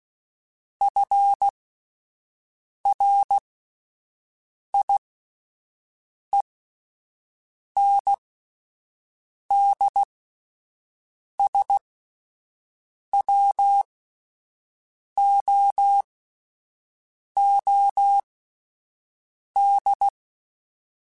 Morse code, heard as a keyed tone, about 16 wpm, spaced as for 5 wpm, 800 Hz. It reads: FRIENDSWOOD
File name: fromSMs / FromSMs mp3s / Cities@5